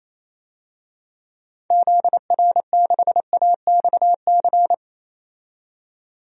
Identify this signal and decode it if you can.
Morse code, heard as a keyed tone, about 28 wpm, 700 Hz. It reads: ZR6AXC
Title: DXCC-100 Call Signs - 6 Characters - 28wpm